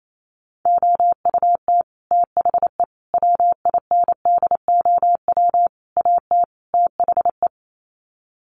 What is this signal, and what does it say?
Morse code, heard as a keyed tone, about 28 wpm, 700 Hz. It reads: OUT THE WINDOW AT THE
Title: Sets of 5 Words Encore2 28wpm